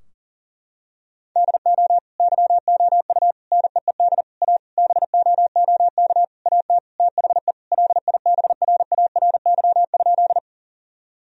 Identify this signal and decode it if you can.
Morse code, heard as a keyed tone, about 40 wpm, 700 Hz. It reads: DO YOU NEED A BOOK AT THE LIBRARY?
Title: Sentences from Top 1000 Words - Review Part 4 - 40wpm